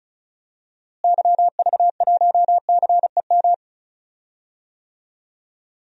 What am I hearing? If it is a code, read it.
Morse code, heard as a keyed tone, about 34 wpm, 700 Hz. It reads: YV1CEM
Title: DXCC-100 Call Signs - 6 Characters - Repeated 3x - 35wpm